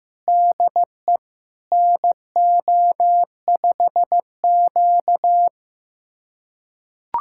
Morse code, heard as a keyed tone, about 16 wpm, 700 Hz. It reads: DE NO5Q
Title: USA Callsign DE Callsign - 4 Characters - 15wpm